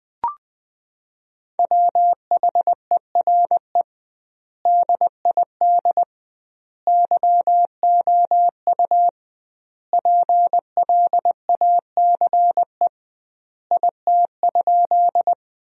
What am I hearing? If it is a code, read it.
Morse code, heard as a keyed tone, about 21 wpm, 700 Hz. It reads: WHERE DID YOU PLACE IT?
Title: 2x Word-Spacing - Sentences from Top 200 Words - 20wpm